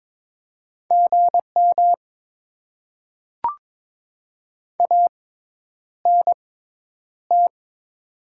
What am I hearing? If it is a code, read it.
Morse code, heard as a keyed tone, about 22 wpm, spaced as for 10 wpm, 690 Hz. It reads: GM ANT
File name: Top 20 QSO Elements - 6x Character Spacing - 22wpm